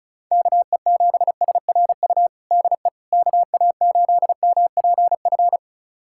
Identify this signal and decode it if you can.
Morse code, heard as a keyed tone, about 36 wpm, 700 Hz. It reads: KE7SRU DE KA8MPF
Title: USA Callsign DE Callsign 35wpm